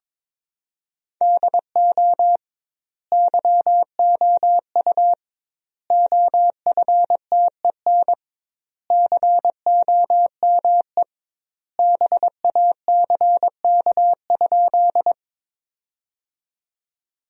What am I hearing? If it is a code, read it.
Morse code, heard as a keyed tone, about 22 wpm, 700 Hz. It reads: DO YOU OFTEN COME BACK?
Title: 2x Word-Spacing - Sentences from Top 300 Words - 22wpm